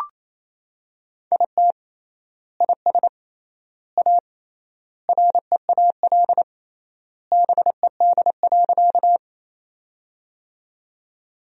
Morse code, heard as a keyed tone, about 28 wpm, 700 Hz. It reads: IT IS A REAL BED.
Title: 3x Word-Spacing - Sentences from Top 500 Words - 28wpm